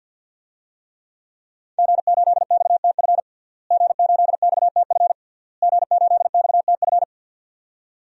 Morse code, heard as a keyed tone, about 52 wpm, 700 Hz. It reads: G8XTF G8XTF G8XTF
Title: DXCC-100 Call Signs - 5 Characters - Repeated 3x - 50wpm